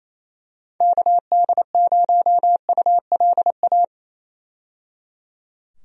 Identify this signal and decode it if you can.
Morse code, heard as a keyed tone, about 28 wpm, 700 Hz. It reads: KD0ULA
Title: USA Call Signs - 6 Characters - 28wpm